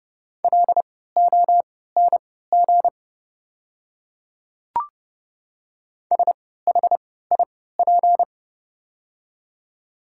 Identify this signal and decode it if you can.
Morse code, heard as a keyed tone, about 30 wpm, spaced as for 17 wpm, 700 Hz. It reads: LONG SHIP
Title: 3x Character Spacing - Top 500 Words - 30wpm